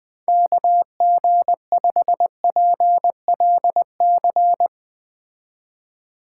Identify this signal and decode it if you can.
Morse code, heard as a keyed tone, about 20 wpm, 700 Hz. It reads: KG5PLC